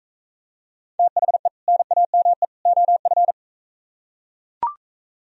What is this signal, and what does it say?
Morse code, heard as a keyed tone, about 42 wpm, 700 Hz. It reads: THE NAME OF